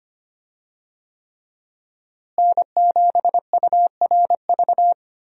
Morse code, heard as a keyed tone, about 25 wpm, 700 Hz. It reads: N7URV
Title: USA Call Signs - 5 Characters - Repeated 3x - 25wpm